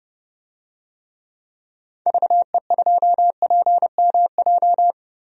Morse code, heard as a keyed tone, about 30 wpm, 690 Hz. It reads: VE2PMJ
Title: DXCC-100 Call Signs - 6 Characters - 30wpm